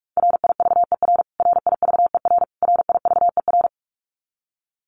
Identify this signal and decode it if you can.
Morse code, heard as a keyed tone, about 44 wpm, 700 Hz. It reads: RIVER RIVER RIVER
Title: Top 500 Words Repeated 45wpm